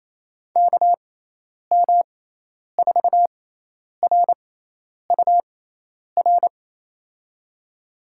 Morse code, heard as a keyed tone, about 28 wpm, spaced as for 10 wpm, 700 Hz. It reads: KM4RUR
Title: USA Call Signs - 6 Characters - 6x Character Spacing - 28wpm